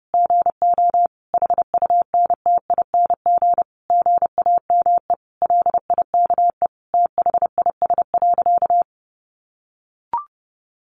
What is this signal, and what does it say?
Morse code, heard as a keyed tone, about 30 wpm, 700 Hz. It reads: GO HUNTING GAME LIKE THIS.
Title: A Sound Of Thunder 30wpm